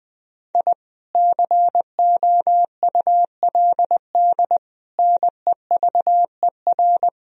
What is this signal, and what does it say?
Morse code, heard as a keyed tone, about 20 wpm, 700 Hz. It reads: I COULD NEVER
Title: Sentences from Top 1000 Words - Review Part 3 - 20wpm